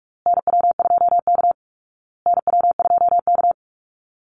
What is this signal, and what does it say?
Morse code, heard as a keyed tone, about 44 wpm, 700 Hz. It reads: NW2K NW2K
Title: Call Signs Repeated 2x - 45wpm